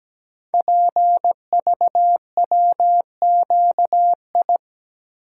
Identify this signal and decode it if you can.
Morse code, heard as a keyed tone, about 17 wpm, 700 Hz. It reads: PVWQI
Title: Code Groups - 5 Letters - 17wpm